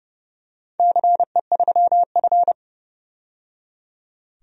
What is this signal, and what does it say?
Morse code, heard as a keyed tone, about 29 wpm, 700 Hz. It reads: CE3F
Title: DXCC-100 Call Signs - 4 Characters - 30wpm